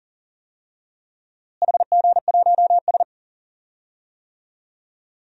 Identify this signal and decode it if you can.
Morse code, heard as a keyed tone, about 40 wpm, 700 Hz. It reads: HG1S
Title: DXCC-100 Call Signs - 4 Characters - Repeated 2x - 40wpm